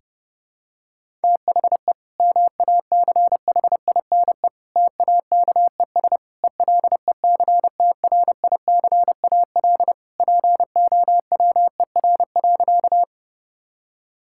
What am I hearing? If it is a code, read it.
Morse code, heard as a keyed tone, about 30 wpm, 700 Hz. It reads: THE MACHINE TAKES ELECTRICAL POWER.
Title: Sentences from Top 1000 Words - Review Part 6 - 30wpm